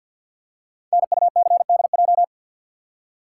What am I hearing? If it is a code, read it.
Morse code, heard as a keyed tone, about 48 wpm, 700 Hz. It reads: NUCDJ
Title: Code Groups - 5 Letters - 50wpm